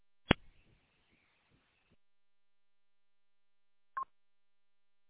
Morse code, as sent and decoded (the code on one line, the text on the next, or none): none